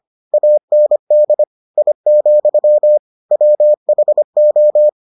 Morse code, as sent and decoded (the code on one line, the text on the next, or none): .- -. -.. / .. --..-- / .-- .... ---
AND I, WHO